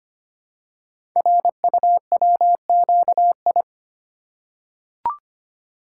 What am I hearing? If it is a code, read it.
Morse code, heard as a keyed tone, about 25 wpm, 700 Hz. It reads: RUWQI
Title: Code Groups - 5 Letters - 25wpm